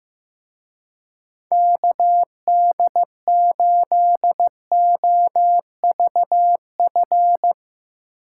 Morse code, heard as a keyed tone, about 15 wpm, 700 Hz. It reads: KD8OVF
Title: Speed Racing - USA 2x3 Call Signs 15wpm